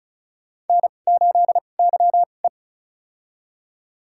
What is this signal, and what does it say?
Morse code, heard as a keyed tone, about 34 wpm, spaced as for 25 wpm, 700 Hz. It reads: N8YE
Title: USA Call Signs - 4 Characters - 2x Character Spacing - 35wpm